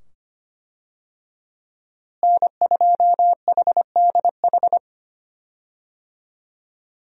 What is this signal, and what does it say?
Morse code, heard as a keyed tone, about 25 wpm, 700 Hz. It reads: N2HDH
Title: USA Call Signs - 5 Characters - Repeated 2x - 25wpm